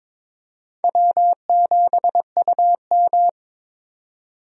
Morse code, heard as a keyed tone, about 22 wpm, 700 Hz. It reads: W7UM
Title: USA Call Signs - 4 Characters - 22wpm